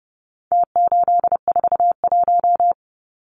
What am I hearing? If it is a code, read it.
Morse code, heard as a keyed tone, about 30 wpm, 700 Hz. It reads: T841
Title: Numbers - Contest Style - 4-Digit 30wpm